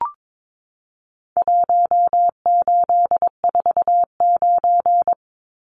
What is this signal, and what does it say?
Morse code, heard as a keyed tone, about 22 wpm, 700 Hz. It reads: 1849
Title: Numbers - Contest Style - 4-Digit 22wpm